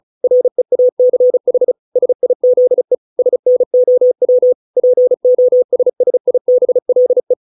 Morse code, heard as a keyed tone, about 34 wpm, 500 Hz. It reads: REACH SIZE SNOW POSSIBLE